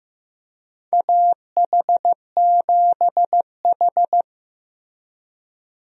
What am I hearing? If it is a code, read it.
Morse code, heard as a keyed tone, about 15 wpm, 700 Hz. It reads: AH7H
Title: USA Call Signs - 4 Characters - Repeated 3x - 15wpm